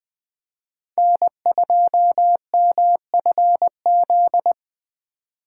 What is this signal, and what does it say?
Morse code, heard as a keyed tone, about 20 wpm, 700 Hz. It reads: N2MFZ